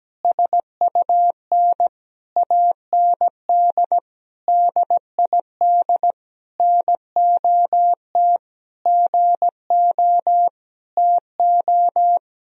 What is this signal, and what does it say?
Morse code, heard as a keyed tone, about 17 wpm, 700 Hz. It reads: SUN AND DID NOT GO TO
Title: Sentences from Top 1000 Words - Review Part 2 - 17wpm